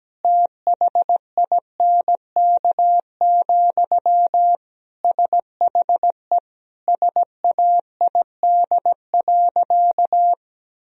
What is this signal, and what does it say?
Morse code, heard as a keyed tone, about 17 wpm, 700 Hz. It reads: THINK, SHE SAID.
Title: Sentences from Top 1000 Words - Review Part 4 - 17wpm